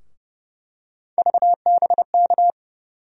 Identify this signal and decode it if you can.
Morse code, heard as a keyed tone, about 30 wpm, 700 Hz. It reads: VBK
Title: USA Call Sign Postfixes - 3 Letters 30wpm